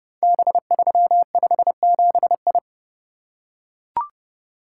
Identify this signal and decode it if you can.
Morse code, heard as a keyed tone, about 30 wpm, 700 Hz. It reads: B357I
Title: Code Groups - 5 Letter:Numbers - 30wpm